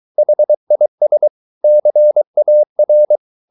Morse code, heard as a keyed tone, about 23 wpm, 600 Hz. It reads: HIS CAR
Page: Romanen Never af Ken Follet på engelsk.